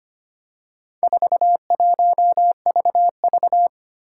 Morse code, heard as a keyed tone, about 25 wpm, 700 Hz. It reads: 41VV
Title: Four Challenging Letter-Number Combinations 25wpm